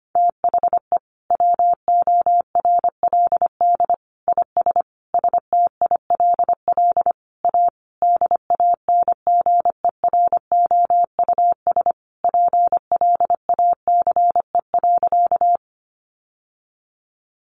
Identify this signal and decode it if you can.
Morse code, heard as a keyed tone, about 25 wpm, 700 Hz. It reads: THE WORLD IS STILL A DANGEROUS PLACE.
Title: Sentences from Top 1200 Words 25wpm